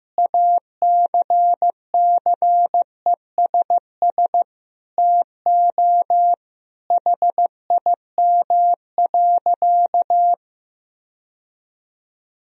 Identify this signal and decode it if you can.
Morse code, heard as a keyed tone, about 15 wpm, 700 Hz. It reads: ACCESS TO HIM.
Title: Sentences from Top 1000 Words - Review Part 4 - 15wpm